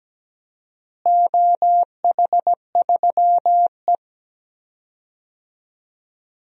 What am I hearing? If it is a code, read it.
Morse code, heard as a keyed tone, about 17 wpm, 700 Hz. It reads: OH3E
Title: DXCC-100 Call Signs - 4 Characters - Repeated 3x - 17wpm